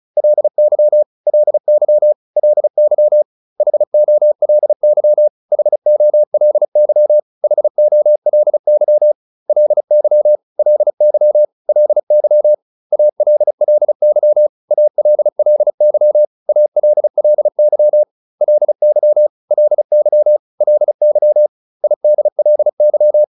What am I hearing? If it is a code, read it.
Morse code, heard as a keyed tone, about 34 wpm, 600 Hz. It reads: LY LY LY HOLY HOLY HOLY LY LY LY ALLY ALLY ALLY LY LY LY IDLY